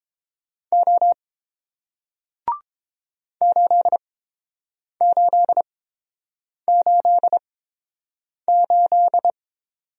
Morse code, heard as a keyed tone, about 29 wpm, spaced as for 7 wpm, 700 Hz. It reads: O 8888